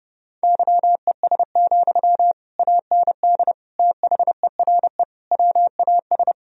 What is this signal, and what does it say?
Morse code, heard as a keyed tone, about 30 wpm, 700 Hz. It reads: YES, AND THERE WAS
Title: Sentences from Top 1000 Words - Review Part 3 - 30wpm